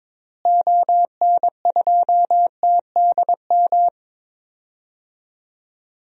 Morse code, heard as a keyed tone, about 22 wpm, 700 Hz. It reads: ON2TDM
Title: DXCC-100 Call Signs - 6 Characters - Repeated 2x - 22wpm